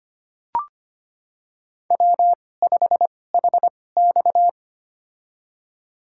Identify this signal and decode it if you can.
Morse code, heard as a keyed tone, about 25 wpm, spaced as for 18 wpm, 700 Hz. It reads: W5HX
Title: USA Call Signs - 4 Characters - 2x Character Spacing - 25wpm